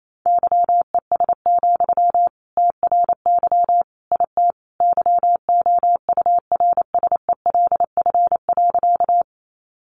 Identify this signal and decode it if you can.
Morse code, heard as a keyed tone, about 28 wpm, 700 Hz. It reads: YES, TRY IT YOURSELF.
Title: Sentences from Top 600 Words 28wpm